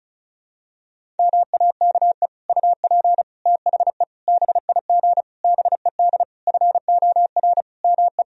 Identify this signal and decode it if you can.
Morse code, heard as a keyed tone, about 34 wpm, 700 Hz. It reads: MAKE UP THE BIG BED FOR ME
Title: Sentences from Top 1000 Words - Review Part 1 - 35wpm